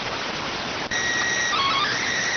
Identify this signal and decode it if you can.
SSTV (slow-scan television), PD 290.